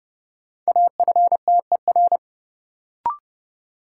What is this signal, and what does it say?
Morse code, heard as a keyed tone, about 30 wpm, 700 Hz. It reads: AFTER
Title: Sentences from Top 200 Words - New Word Drill - 30wpm